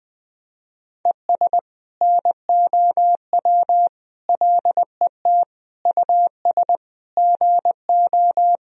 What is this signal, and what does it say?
Morse code, heard as a keyed tone, about 20 wpm, 700 Hz. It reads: ES NOW LET US GO
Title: Sentences from Top 300 Words and Abbreviations 20wpm